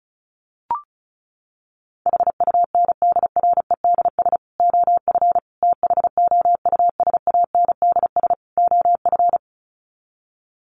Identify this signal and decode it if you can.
Morse code, heard as a keyed tone, about 34 wpm, 700 Hz. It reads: HUNDREDS OF THOUSANDS OF
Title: Sets of 4 Words 35wpm